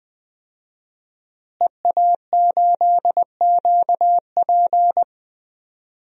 Morse code, heard as a keyed tone, about 20 wpm, 700 Hz. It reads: EA8QP
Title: DXCC-100 Call Signs - 5 Characters - Repeated 3x - 20wpm